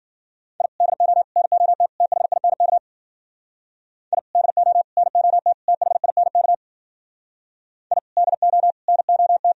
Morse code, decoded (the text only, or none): I DO NOT THINK I DO NOT THINK I DO NOT